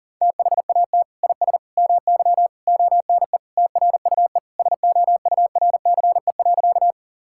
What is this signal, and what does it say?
Morse code, heard as a keyed tone, about 40 wpm, 700 Hz. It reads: THAT IS MY ONE TRUE SOURCE.
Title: Sentences from Top 1000 Words - Review Part 5 - 40wpm